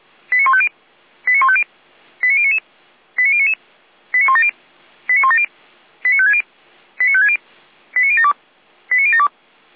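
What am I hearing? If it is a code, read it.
Five-tone selective call (ZVEI), heard as quick runs of five tones, tones of about 70 ms. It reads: 89150 89150 89099 89099 89170 89170 89560 89560 89072 89072
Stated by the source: ZVEI five-tone selective call received on an SDR.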